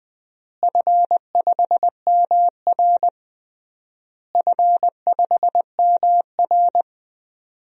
Morse code, heard as a keyed tone, about 21 wpm, 700 Hz. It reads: F5MR F5MR
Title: DXCC-100 Call Signs - 4 Characters - Repeated 3x - 20wpm